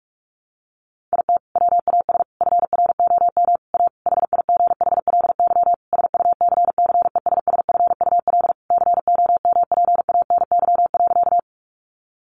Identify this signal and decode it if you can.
Morse code, heard as a keyed tone, about 44 wpm, 700 Hz. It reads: IT WAS FROM A HIGHLY SUCCESSFUL COMPANY.
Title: Sentences from Top 1100 Words 45wpm